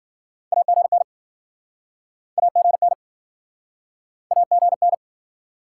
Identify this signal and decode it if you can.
Morse code, heard as a keyed tone, about 56 wpm, 700 Hz. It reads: AGN AGN AGN